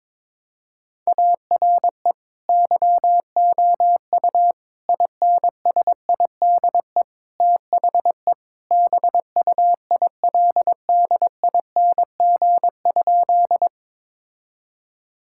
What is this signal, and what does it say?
Morse code, heard as a keyed tone, about 22 wpm, 700 Hz. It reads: ARE YOU INSIDE THE BUILDING?
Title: Sentences from Top 1000 Words - Review Part 6 - 22wpm